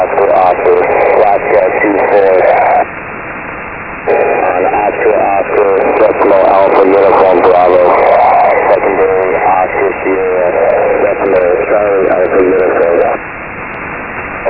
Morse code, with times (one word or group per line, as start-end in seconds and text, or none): none